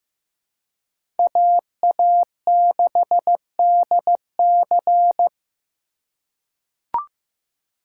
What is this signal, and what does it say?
Morse code, heard as a keyed tone, about 15 wpm, 700 Hz. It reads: AA6DC
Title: USA Call Signs - 5 Characters - 2x Character Spacing - 15wpm